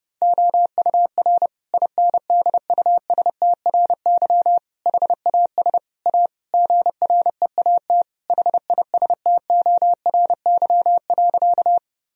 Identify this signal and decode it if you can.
Morse code, heard as a keyed tone, about 30 wpm, 700 Hz. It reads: OUR INDUSTRY HAS A GREAT HISTORY.
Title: Sentences from Top 1000 Words - Review Part 4 - 30wpm